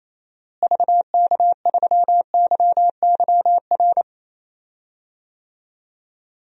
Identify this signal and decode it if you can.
Morse code, heard as a keyed tone, about 28 wpm, 700 Hz. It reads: VK3YYR
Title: DXCC-100 Call Signs - 6 Characters - Repeated 2x - 28wpm